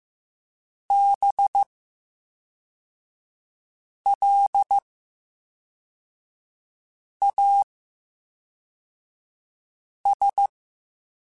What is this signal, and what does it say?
Morse code, heard as a keyed tone, about 15 wpm, 800 Hz. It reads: B L A S
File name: Special INGs / Special INGs-mp3s / VerySlowToFaster Ing @10